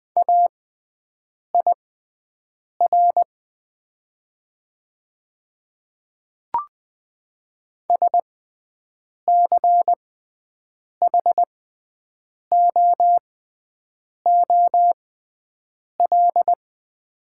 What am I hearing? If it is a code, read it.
Morse code, heard as a keyed tone, about 20 wpm, spaced as for 7 wpm, 700 Hz. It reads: AIR SCHOOL
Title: 6x Character Spacing - Top 300 Words - 20wpm